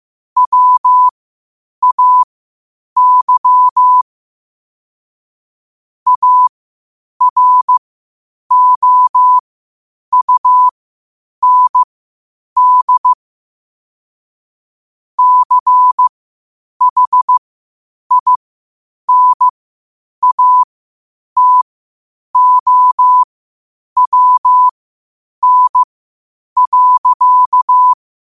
Morse code, heard as a keyed tone, about 15 wpm, spaced as for 8 wpm, 1000 Hz. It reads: WAY AROUND CHINATOWN.
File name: qotdfiles / 8wpm